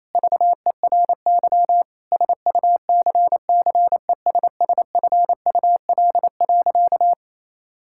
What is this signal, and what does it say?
Morse code, heard as a keyed tone, about 28 wpm, 700 Hz. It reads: VERY SUCCESSFUL.
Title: Sentences from Top 1000 Words - Review Part 6 - 28wpm